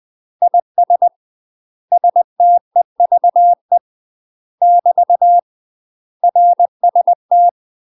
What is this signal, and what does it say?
Morse code, heard as a keyed tone, about 20 wpm, 700 Hz. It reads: IS STEVE <BT> RST
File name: UK MASTER QSOs TRAINING / 20WPM / UK MASTER QSOs 0071